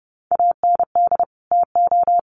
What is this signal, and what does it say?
Morse code, heard as a keyed tone, about 30 wpm, 700 Hz. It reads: AND TO
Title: Sets of 2 Words 30wpm